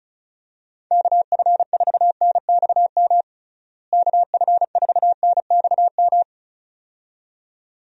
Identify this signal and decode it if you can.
Morse code, heard as a keyed tone, about 34 wpm, 700 Hz. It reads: KF4NXM KF4NXM
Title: USA Call Signs - 6 Characters - Repeated 2x - 35wpm